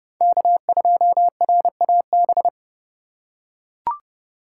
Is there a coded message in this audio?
Yes: Morse code from 0.2 to 2.5 s, heard as a keyed tone, about 28 wpm, 700 Hz. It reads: K2RAB